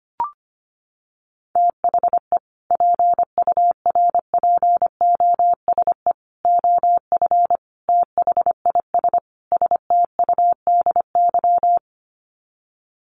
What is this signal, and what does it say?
Morse code, heard as a keyed tone, about 25 wpm, 700 Hz. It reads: THE PURPOSE OF THIS STUDY
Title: Sets of 5 Words 25wpm